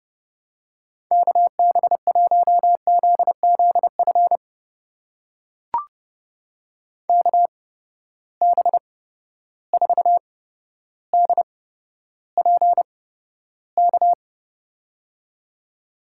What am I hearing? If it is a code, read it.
Morse code, heard as a keyed tone, about 32 wpm, spaced as for 13 wpm, 700 Hz. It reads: KB1ZZF KB4DPK
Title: USA Call Signs - 6 Characters - 8x Character Spacing - 30wpm